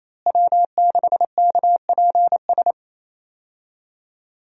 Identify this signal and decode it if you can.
Morse code, heard as a keyed tone, about 27 wpm, 700 Hz. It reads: W6KPS
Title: USA Call Signs - 5 Characters - Repeated 2x - 28wpm